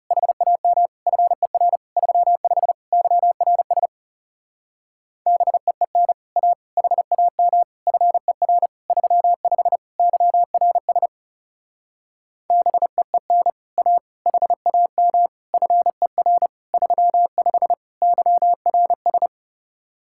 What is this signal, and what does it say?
Morse code, heard as a keyed tone, about 37 wpm, 700 Hz. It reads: HAM FER 35 YRS BEEN A HAM FER 35 YRS BEEN A HAM FER 35 YRS
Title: Speed-Racing Common QSO Phrases - 30wpm